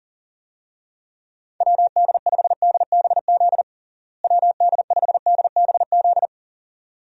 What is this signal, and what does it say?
Morse code, heard as a keyed tone, about 40 wpm, 700 Hz. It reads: WD5DBZ WD5DBZ
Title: USA Call Signs - 6 Characters - Repeated 2x - 40wpm